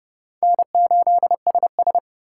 Morse code, heard as a keyed tone, about 30 wpm, 700 Hz. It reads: N8SS
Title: K1USN SST QSO Practice - 30wpm